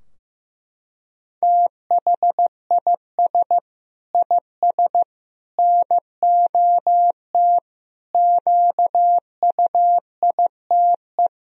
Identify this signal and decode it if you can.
Morse code, heard as a keyed tone, about 15 wpm, 700 Hz. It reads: THIS IS NOT QUITE
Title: Sentences from Top 1000 Words - Review Part 2 - 15wpm